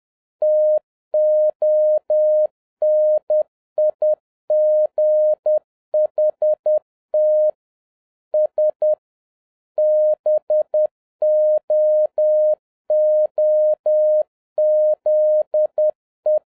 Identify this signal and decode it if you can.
Morse code, heard as a keyed tone, about 10 wpm, 610 Hz. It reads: TONIGHT S BOOZE
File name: 2011-07-mp3 / n31 THE 3.11 2 10wpm